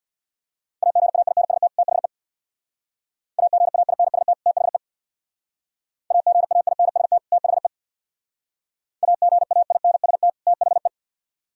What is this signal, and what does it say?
Morse code, heard as a keyed tone, about 61 wpm, 690 Hz. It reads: AGAINST THE AGAINST THE AGAINST THE AGAINST THE
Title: Speed-Racing - Sets of 2 Words 50wpm